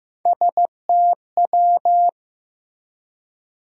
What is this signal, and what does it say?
Morse code, heard as a keyed tone, about 16 wpm, 700 Hz. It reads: STW
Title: USA Call Sign Postfixes - 3 Letters 15wpm